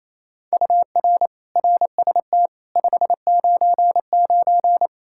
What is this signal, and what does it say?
Morse code, heard as a keyed tone, about 28 wpm, 700 Hz. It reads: UR RST 599
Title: Practice QSO - Exchange 1 - Beginning - 28wpm